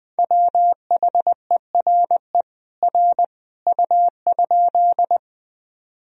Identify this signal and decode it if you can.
Morse code, heard as a keyed tone, about 20 wpm, 700 Hz. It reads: WHERE R U?
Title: Sentences from Top 200 Words and Abbreviations 20wpm